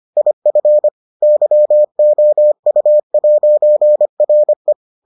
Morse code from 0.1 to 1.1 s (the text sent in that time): IF